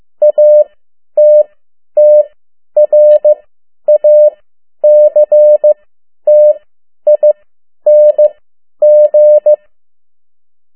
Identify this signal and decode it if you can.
Morse code, heard as a keyed tone, about 15 wpm, spaced as for 10 wpm, 600 Hz. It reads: ATTRACTING
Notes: Morse code 10 WPM